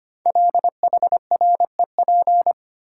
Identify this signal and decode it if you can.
Morse code, heard as a keyed tone, about 25 wpm, 700 Hz. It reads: LHREP
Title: Code Groups - 5 Letters - 25wpm